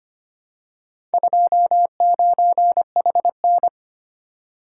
Morse code, heard as a keyed tone, about 25 wpm, 700 Hz. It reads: 29HN
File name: practice / 25wpm / Four Letter-Number Combinations - 25wpm